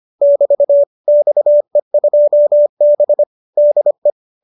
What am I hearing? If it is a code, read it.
Morse code, heard as a keyed tone, about 25 wpm, 590 Hz. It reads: <BT> XE2B DE